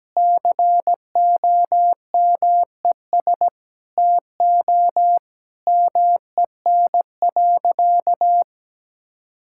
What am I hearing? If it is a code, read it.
Morse code, heard as a keyed tone, about 17 wpm, 700 Hz. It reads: COMES TO MEN.
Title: Sentences from Top 1000 Words - Review Part 6 - 17wpm